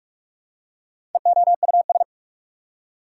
Morse code, heard as a keyed tone, about 47 wpm, 690 Hz. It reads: EOUS
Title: Prefixes and Suffixes 45wpm